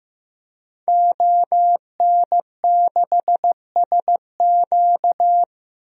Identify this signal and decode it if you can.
Morse code, heard as a keyed tone, about 16 wpm, 700 Hz. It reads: ON6SQ
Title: DXCC-100 Call Signs - 5 Characters - Repeated 3x - 15wpm